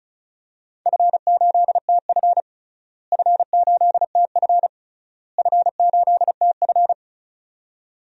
Morse code, heard as a keyed tone, about 36 wpm, 700 Hz. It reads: F8TF F8TF F8TF
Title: DXCC-100 Call Signs - 4 Characters - Repeated 3x - 35wpm